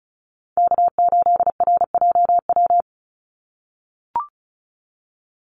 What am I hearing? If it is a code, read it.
Morse code, heard as a keyed tone, about 34 wpm, 700 Hz. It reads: K8RJW
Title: Call Signs - 35wpm